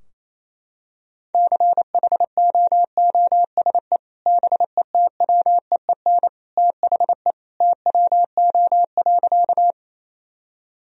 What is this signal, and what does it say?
Morse code, heard as a keyed tone, about 28 wpm, 700 Hz. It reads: CHOOSE BETWEEN THE TWO.
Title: Sentences from Top 1000 Words - Review Part 4 - 28wpm